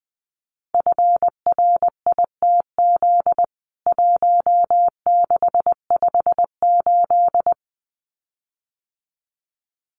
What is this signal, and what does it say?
Morse code, heard as a keyed tone, about 20 wpm, 700 Hz. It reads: FRITZ 1658
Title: CWT Exchange 20wpm